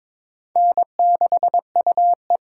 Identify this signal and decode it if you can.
Morse code, heard as a keyed tone, about 23 wpm, 700 Hz. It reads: N6UE
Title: USA Call Signs - 4 Characters - 8x Character Spacing - 22wpm